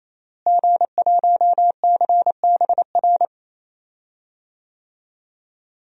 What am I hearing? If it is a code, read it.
Morse code, heard as a keyed tone, about 28 wpm, 700 Hz. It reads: G1CBR
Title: DXCC-100 Call Signs - 5 Characters - Repeated 3x - 28wpm